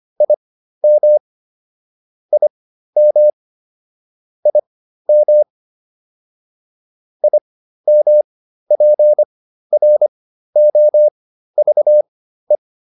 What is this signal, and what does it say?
Morse code, heard as a keyed tone, about 25 wpm, spaced as for 13 wpm, 600 Hz. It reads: IM IM IM IMPROVE
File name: IM7 13